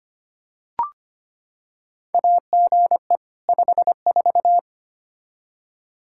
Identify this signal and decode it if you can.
Morse code, heard as a keyed tone, about 25 wpm, 700 Hz. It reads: AGE 54
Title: Practice QSO - Exchange 3 - More Info - 25wpm